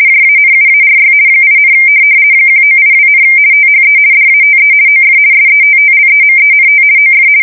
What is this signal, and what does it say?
RTTY, about 50 baud. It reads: JANUARY 8, 1982, THE DATE OF SETTLEMENT OF
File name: fsk170-50B / teletype-corp